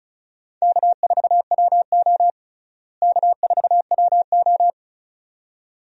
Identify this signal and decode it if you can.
Morse code, heard as a keyed tone, about 36 wpm, 700 Hz. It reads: K4WO K4WO
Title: USA Call Signs - 4 Characters - Repeated 2x - 35wpm